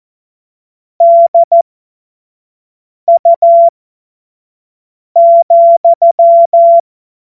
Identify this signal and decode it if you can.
Morse code, heard as a keyed tone, about 14 wpm, spaced as for 5 wpm, 680 Hz. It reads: DU,
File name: Morsekurs / Witze / Witz 19. Oktober 2021, 25 F70